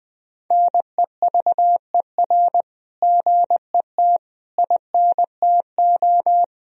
Morse code, heard as a keyed tone, about 20 wpm, 700 Hz. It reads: NEVER GET INTO
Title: Sentences from Top 1000 Words - Review Part 3 - 20wpm